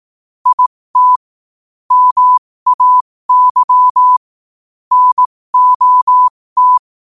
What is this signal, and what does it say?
Morse code, heard as a keyed tone, about 18 wpm, spaced as for 15 wpm, 1000 Hz. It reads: IT MAY NOT
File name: qotdfiles / 15wpm